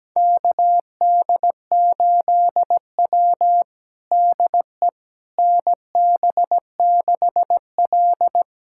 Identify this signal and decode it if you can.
Morse code, heard as a keyed tone, about 17 wpm, 700 Hz. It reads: KD8W DE NB6L
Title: USA Callsign DE Callsign - 4 Characters - 17wpm